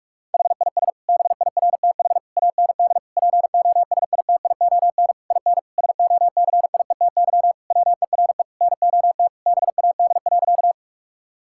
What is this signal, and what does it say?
Morse code, heard as a keyed tone, about 46 wpm, 700 Hz. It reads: HIS BIRTH AND POSITION IN SOCIETY WERE NOT BAD.
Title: Sentences from Top 1000 Words - Review Part 5 - 45wpm